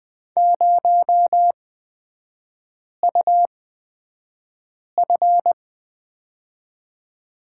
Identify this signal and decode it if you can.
Morse code, heard as a keyed tone, about 20 wpm, spaced as for 5 wpm, 700 Hz. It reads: 0UF
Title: 236 Lesson 38e - Letter q - Three Characters 20wpm